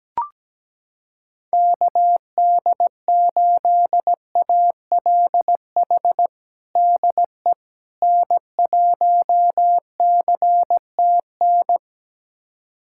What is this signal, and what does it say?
Morse code, heard as a keyed tone, about 17 wpm, 700 Hz. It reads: KD8ALH DE N1CTN
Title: USA Callsign DE Callsign 17wpm